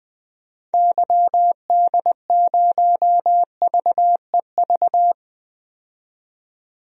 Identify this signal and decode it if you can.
Morse code, heard as a keyed tone, about 21 wpm, 700 Hz. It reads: YD0VEV
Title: DXCC-100 Call Signs - 6 Characters - 20wpm